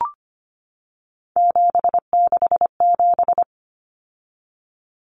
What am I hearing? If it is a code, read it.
Morse code, heard as a keyed tone, about 25 wpm, 700 Hz. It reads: 767
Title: Numbers - 3-Digit 25wpm